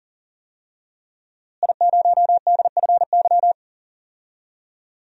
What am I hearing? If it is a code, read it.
Morse code, heard as a keyed tone, about 40 wpm, 700 Hz. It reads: I0DFY
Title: DXCC-100 Call Signs - 5 Characters - Repeated 3x - 40wpm